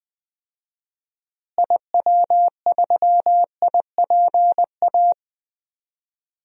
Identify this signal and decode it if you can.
Morse code, heard as a keyed tone, about 20 wpm, 700 Hz. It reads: IW3IPA
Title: DXCC-100 Call Signs - 6 Characters - Repeated 3x - 20wpm